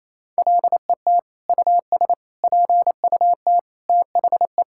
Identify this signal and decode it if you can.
Morse code, heard as a keyed tone, about 28 wpm, 700 Hz. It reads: LET US PUT THE
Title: Sentences from Top 1000 Words - Review Part 5 - 28wpm